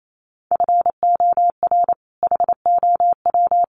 Morse code, heard as a keyed tone, about 28 wpm, 700 Hz. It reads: FOR HOW
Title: Sentences from Top 200 Words 28wpm